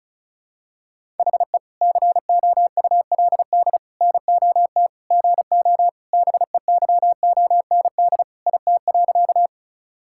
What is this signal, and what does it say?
Morse code, heard as a keyed tone, about 36 wpm, 700 Hz. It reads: HE COULD NOT GO BEYOND IT.
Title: Sentences from Top 1000 Words - Review Part 6 - 35wpm